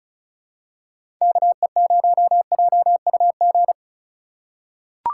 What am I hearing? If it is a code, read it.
Morse code, heard as a keyed tone, about 34 wpm, 690 Hz. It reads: KE0JUG
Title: USA Call Signs - 4x Character Spacing - 35wpm